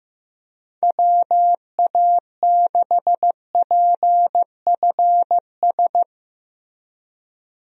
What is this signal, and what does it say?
Morse code, heard as a keyed tone, about 15 wpm, 700 Hz. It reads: WA6PFS
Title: USA Call Signs - 6 Characters - Repeated 2x - 15wpm